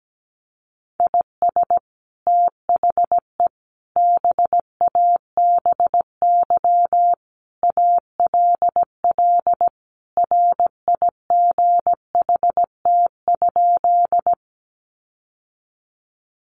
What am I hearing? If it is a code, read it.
Morse code, heard as a keyed tone, about 17 wpm, 700 Hz. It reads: IS THE BABY ALL RIGHT?
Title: Sentences from Top 1300 Words 17wpm